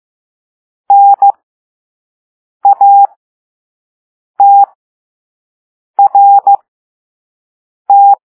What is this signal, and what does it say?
Morse code, heard as a keyed tone, about 15 wpm, spaced as for 5 wpm, 800 Hz. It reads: NATRT